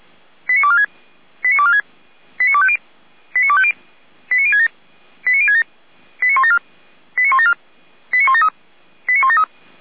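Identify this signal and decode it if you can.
Five-tone selective call (ZVEI), heard as quick runs of five tones, tones of about 70 ms. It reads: 89246 89246 89250 89250 89067 89067 89174 89174 89173 89173
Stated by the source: ZVEI five-tone selective call received on an SDR by Feuerwehr Orlamünde.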